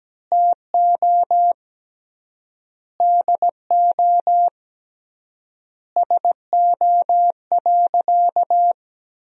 Morse code, heard as a keyed tone, about 17 wpm, 700 Hz. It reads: TO DO SO.
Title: 3x Word-Spacing - Sentences from Top 500 Words - 17wpm